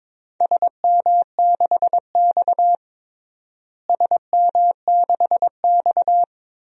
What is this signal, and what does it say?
Morse code, heard as a keyed tone, about 22 wpm, 700 Hz. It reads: SM6X SM6X
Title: DXCC-100 Call Signs - 4 Characters - Repeated 2x - 22wpm